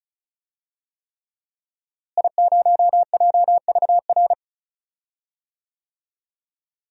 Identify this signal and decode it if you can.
Morse code, heard as a keyed tone, about 36 wpm, 700 Hz. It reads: I0JVR
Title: DXCC-100 Call Signs - 5 Characters - Repeated 3x - 35wpm